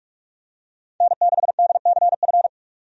Morse code, heard as a keyed tone, about 47 wpm, 700 Hz. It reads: N6DCF
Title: USA Call Signs - 5 Characters - 45wpm